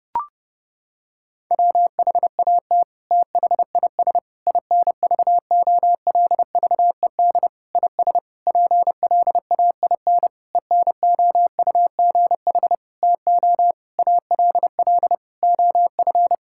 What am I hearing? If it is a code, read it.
Morse code, heard as a keyed tone, about 32 wpm, 700 Hz. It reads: WHAT THIS INVOLVED IS PLAIN ENOUGH TO ALL OF
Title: Sentences from Top 1000 Words - Review Part 5 - 30wpm